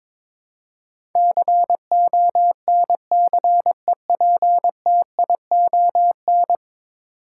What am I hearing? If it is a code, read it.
Morse code, heard as a keyed tone, about 21 wpm, 700 Hz. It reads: CONCEPTION
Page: morse-code-ninja-practice-22wpm